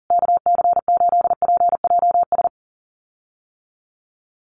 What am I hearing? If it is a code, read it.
Morse code, heard as a keyed tone, about 40 wpm, 700 Hz. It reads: KC8PJS
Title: USA Call Signs Repeated 3x 40wpm